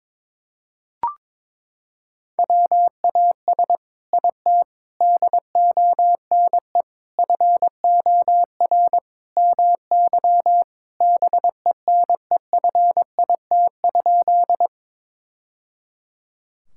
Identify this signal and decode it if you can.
Morse code, heard as a keyed tone, about 22 wpm, 700 Hz. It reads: WAS IT DONE FOR MY BENEFIT?
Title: Sentences from Top 1000 Words - Review Part 6 - 22wpm